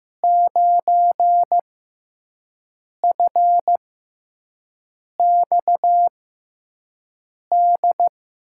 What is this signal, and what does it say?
Morse code, heard as a keyed tone, about 15 wpm, spaced as for 5 wpm, 700 Hz. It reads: 9FXD
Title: USA Call Signs - 6 Characters - 6x Character Spacing - 15wpm